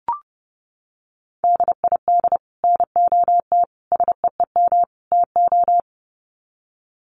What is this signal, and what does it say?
Morse code, heard as a keyed tone, about 30 wpm, 700 Hz. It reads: DID NOT SEEM TO
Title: Sets of 4 Words 30wpm